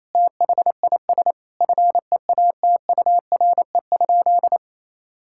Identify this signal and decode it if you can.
Morse code, heard as a keyed tone, about 28 wpm, 700 Hz. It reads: THIS FEATURE?
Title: Sentences from Top 1000 Words - Review Part 5 - 28wpm